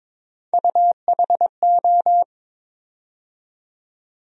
Morse code, heard as a keyed tone, about 22 wpm, 700 Hz. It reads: UHO
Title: USA Call Sign Postfixes - 3 Letters 22wpm